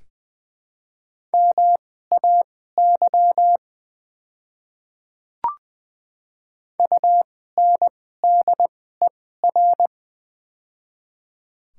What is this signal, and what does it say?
Morse code, heard as a keyed tone, about 20 wpm, spaced as for 14 wpm, 700 Hz. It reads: MAY UNDER
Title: 2x Character Spacing - Top 500 Words - 20wpm